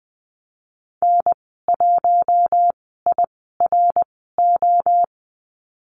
Morse code, heard as a keyed tone, about 20 wpm, spaced as for 14 wpm, 700 Hz. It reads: N1IRO
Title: 062 Lesson 11g - Letter h - Call Signs 20wpm